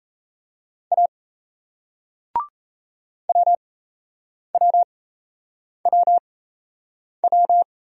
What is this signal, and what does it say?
Morse code, heard as a keyed tone, about 36 wpm, spaced as for 8 wpm, 700 Hz. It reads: A WWWW